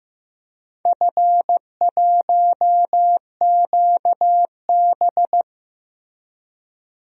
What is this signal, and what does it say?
Morse code, heard as a keyed tone, about 16 wpm, 700 Hz. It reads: F1QB
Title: DXCC-100 Call Signs - 4 Characters - Repeated 2x - 15wpm